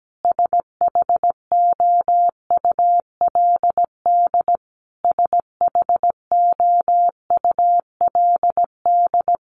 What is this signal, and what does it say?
Morse code, heard as a keyed tone, about 17 wpm, 700 Hz. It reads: SHOULD SHOULD
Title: Top 500 Words Repeated 17wpm